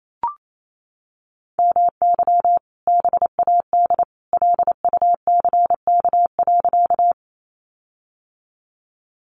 Morse code, heard as a keyed tone, about 28 wpm, 700 Hz. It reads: MY BAD LUCK.
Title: Sentences from Top 1500 Words 28wpm